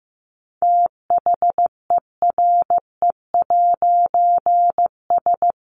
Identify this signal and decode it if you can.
Morse code, heard as a keyed tone, about 15 wpm, 700 Hz. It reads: THERE'S
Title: A Sound Of Thunder 15wpm